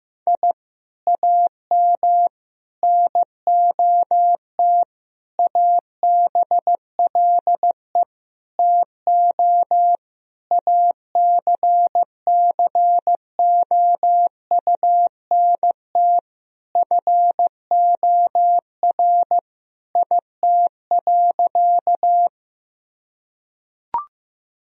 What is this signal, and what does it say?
Morse code, heard as a keyed tone, about 15 wpm, 700 Hz. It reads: I AM NOT ABLE TO ACCOUNT FOR IT.
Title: Sentences from Top 1000 Words - Review Part 2 - 15wpm